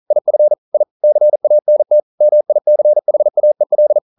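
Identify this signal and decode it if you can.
Morse code, heard as a keyed tone, about 40 wpm, 600 Hz. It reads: IF I CANT MICHAEL
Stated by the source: Morse code practice at 41 wpm, from CW med Gnister.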